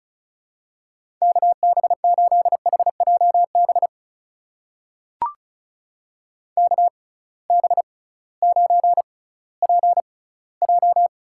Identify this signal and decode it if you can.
Morse code, heard as a keyed tone, about 36 wpm, spaced as for 19 wpm, 700 Hz. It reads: KB8HJB KB9PJ
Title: USA Call Signs - 6 Characters - 6x Character Spacing - 35wpm